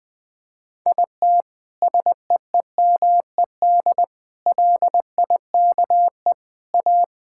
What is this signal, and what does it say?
Morse code, heard as a keyed tone, about 20 wpm, 700 Hz. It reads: IT SEEMED LIKE A
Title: Sets of 9 Words - 20wpm